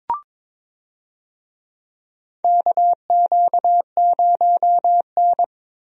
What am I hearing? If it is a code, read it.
Morse code, heard as a keyed tone, about 22 wpm, 700 Hz. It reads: KQ0N
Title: USA Call Signs - 4 Characters - 22wpm